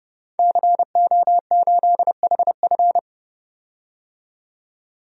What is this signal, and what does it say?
Morse code, heard as a keyed tone, about 32 wpm, 700 Hz. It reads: CO8HF
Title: DXCC-100 Call Signs - 5 Characters - 30wpm